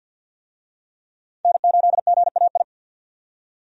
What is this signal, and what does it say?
Morse code, heard as a keyed tone, about 51 wpm, 700 Hz. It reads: N8GAI
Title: USA Call Signs - 5 Characters - 50wpm